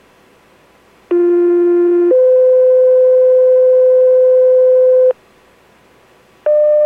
Two-tone page, heard as a one-second tone, then a three-second tone, about 350 then 510 Hz. It is heard from 1.1 to 5.1 s.